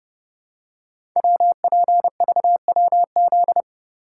Morse code, heard as a keyed tone, about 29 wpm, 700 Hz. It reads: WPVWZ
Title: Code Groups - 5 Letters - 30wpm